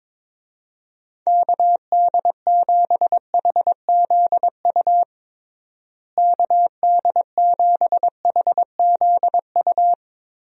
Morse code, heard as a keyed tone, about 22 wpm, 700 Hz. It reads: KD7HZU KD7HZU
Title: USA Call Signs - 6 Characters - Repeated 3x - 22wpm